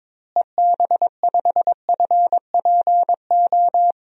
Morse code, heard as a keyed tone, about 22 wpm, 700 Hz. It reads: EB5FPO
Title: DXCC-100 Call Signs - 6 Characters - Repeated 2x - 22wpm